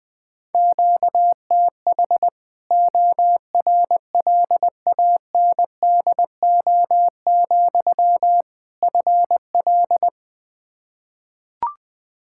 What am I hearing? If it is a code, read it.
Morse code, heard as a keyed tone, about 20 wpm, 700 Hz. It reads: QTH ORLANDO, FL
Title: Practice QSO - Exchange 1 - Beginning - 20wpm